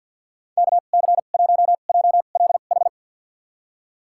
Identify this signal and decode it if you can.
Morse code, heard as a keyed tone, about 50 wpm, spaced as for 36 wpm, 700 Hz. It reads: KC1JLH
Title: USA Call Signs - 2x Character Spacing - 50wpm